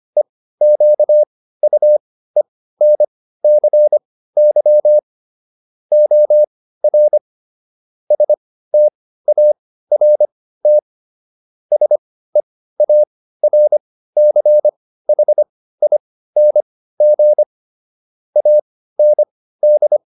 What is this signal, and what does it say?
Morse code, heard as a keyed tone, about 25 wpm, spaced as for 15 wpm, 600 Hz. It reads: EQUENCY OR START SEARCHING AND